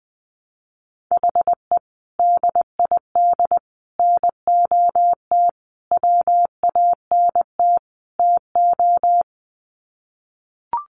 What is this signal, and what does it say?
Morse code, heard as a keyed tone, about 20 wpm, 700 Hz. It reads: HE DID NOT WANT TO
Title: Sets of 5 Words Encore 20wpm